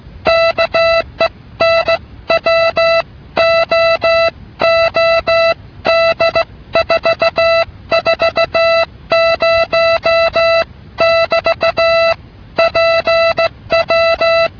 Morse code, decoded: KENWOOD 440 <BT> PW